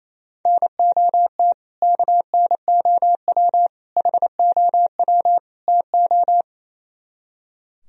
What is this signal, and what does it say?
Morse code, heard as a keyed tone, about 35 wpm, 700 Hz. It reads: NOT KNOW HOW TO